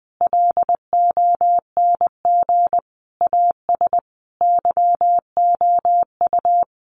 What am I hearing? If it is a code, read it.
Morse code, heard as a keyed tone, about 20 wpm, 700 Hz. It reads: LONG AS YOU
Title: Sets of 4 Words 20wpm